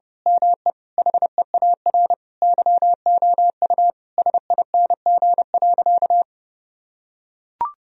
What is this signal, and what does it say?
Morse code, heard as a keyed tone, about 30 wpm, 700 Hz. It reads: ME HEAR YOU SING.
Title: Sentences from Top 1000 Words - Review Part 2 - 30wpm